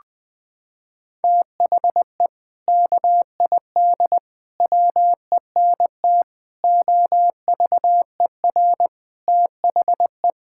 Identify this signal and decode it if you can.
Morse code, heard as a keyed tone, about 21 wpm, 700 Hz. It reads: THE KID WENT OVER THE
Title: Sentences from Top 1600 Words 20wpm